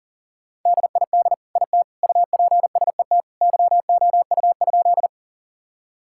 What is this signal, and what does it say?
Morse code, heard as a keyed tone, about 40 wpm, 700 Hz. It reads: DID IT UPSET YOU?
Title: Sentences from Top 1000 Words - Review Part 5 - 40wpm